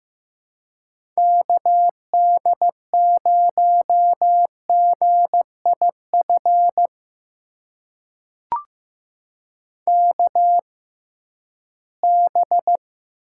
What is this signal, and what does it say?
Morse code, heard as a keyed tone, about 15 wpm, spaced as for 11 wpm, 700 Hz. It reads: KD0GIF KB
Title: USA Call Signs - 6 Characters - 6x Character Spacing - 15wpm